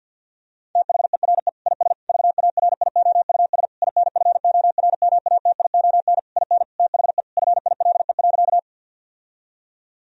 Morse code, heard as a keyed tone, about 50 wpm, 700 Hz. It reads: THERE IS VARIOUS INFORMATION IN THE FILE.
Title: Sentences from Top 1000 Words - Review Part 4 - 50wpm